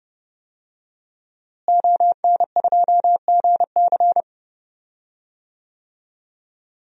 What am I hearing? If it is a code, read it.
Morse code, heard as a keyed tone, about 29 wpm, 700 Hz. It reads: ON2GC
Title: DXCC-100 Call Signs - 5 Characters - Repeated 3x - 30wpm